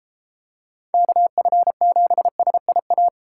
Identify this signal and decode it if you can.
Morse code, heard as a keyed tone, about 32 wpm, 700 Hz. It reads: KF7SIA